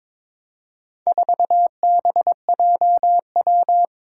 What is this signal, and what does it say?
Morse code, heard as a keyed tone, about 22 wpm, 700 Hz. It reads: 4BJW
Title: Four Challenging Letter-Number Combinations 22wpm